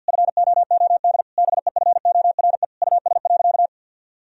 Morse code, heard as a keyed tone, about 52 wpm, 700 Hz. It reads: FOOD BEFORE US.
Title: Sentences from Top 1000 Words - Review Part 4 - 50wpm